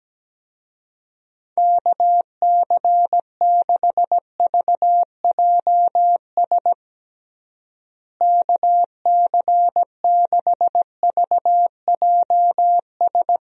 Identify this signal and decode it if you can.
Morse code, heard as a keyed tone, about 17 wpm, 700 Hz. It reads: KC6VJS KC6VJS
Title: USA Call Signs - 6 Characters - Repeated 2x - 17wpm